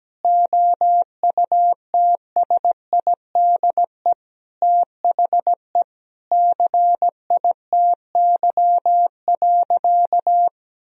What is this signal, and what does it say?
Morse code, heard as a keyed tone, about 17 wpm, 700 Hz. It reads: OUTSIDE THE CITY.
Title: Sentences from Top 1000 Words - Review Part 4 - 17wpm